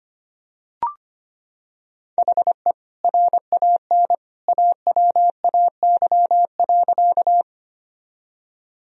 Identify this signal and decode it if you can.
Morse code, heard as a keyed tone, about 25 wpm, 700 Hz. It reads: HE RAN AWAY.
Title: Sentences from Top 1000 Words - Review Part 3 - 25wpm